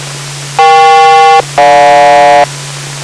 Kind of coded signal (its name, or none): aircraft SELCAL